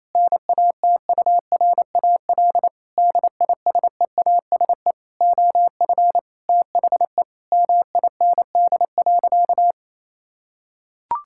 Morse code, heard as a keyed tone, about 28 wpm, 700 Hz. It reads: NATURAL DISEASE OF THE MIND.
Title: Sentences from Top 1000 Words - Review Part 6 - 28wpm